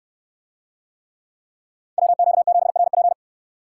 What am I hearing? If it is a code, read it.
Morse code, heard as a keyed tone, about 69 wpm, 700 Hz. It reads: WQ7RP